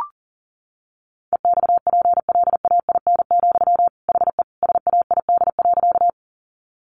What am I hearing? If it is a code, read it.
Morse code, heard as a keyed tone, about 40 wpm, 700 Hz. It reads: EXPLAIN, HE SAID.
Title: Sentences from Top 1100 Words 40wpm